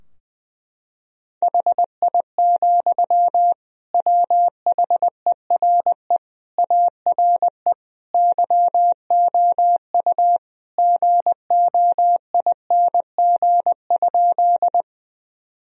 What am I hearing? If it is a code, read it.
Morse code, heard as a keyed tone, about 20 wpm, 700 Hz. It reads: HI, WHERE ARE YOU GOING?
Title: Sentences from Top 1000 Words - Review Part 3 - 20wpm